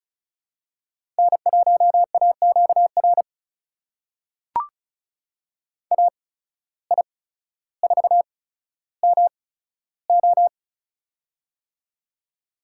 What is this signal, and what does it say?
Morse code, heard as a keyed tone, about 36 wpm, spaced as for 15 wpm, 700 Hz. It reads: N1AQR AI4MO
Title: USA Call Signs - 8x Character Spacing - 35wpm